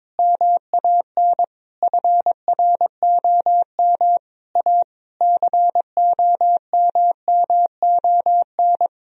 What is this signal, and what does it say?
Morse code, heard as a keyed tone, about 22 wpm, 700 Hz. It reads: MAN FROM A COMMON
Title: Sentences from Top 1000 Words - Review Part 4 - 22wpm